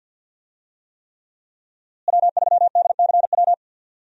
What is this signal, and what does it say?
Morse code, heard as a keyed tone, about 51 wpm, 700 Hz. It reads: W3DCW
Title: USA Call Signs - 5 Characters - Repeated 3x - 50wpm